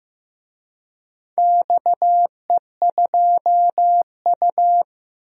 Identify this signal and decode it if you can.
Morse code, heard as a keyed tone, about 15 wpm, 700 Hz. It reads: XE2U
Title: DXCC-100 Call Signs - 4 Characters - Repeated 3x - 15wpm